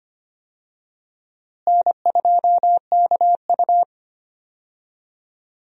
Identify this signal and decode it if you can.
Morse code, heard as a keyed tone, about 25 wpm, 700 Hz. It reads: N2KU
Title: USA Call Signs - 4 Characters - Repeated 2x - 25wpm